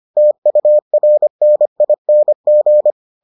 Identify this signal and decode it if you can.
Morse code, heard as a keyed tone, about 25 wpm, 600 Hz. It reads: TURNING